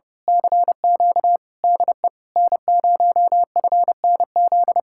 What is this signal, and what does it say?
Morse code, heard as a keyed tone, about 28 wpm, 700 Hz. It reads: CQ DE N0FNZ
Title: Speed-Racing Common QSO Phrases - 30wpm